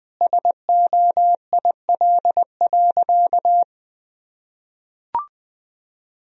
Morse code, heard as a keyed tone, about 20 wpm, 700 Hz. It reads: SOIL.
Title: Sentences from Top 1000 Words - Review Part 4 - 20wpm